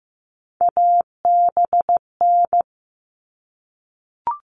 Morse code, heard as a keyed tone, about 15 wpm, 700 Hz. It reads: ABN
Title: All Cut Numbers - 3-Digit 15wpm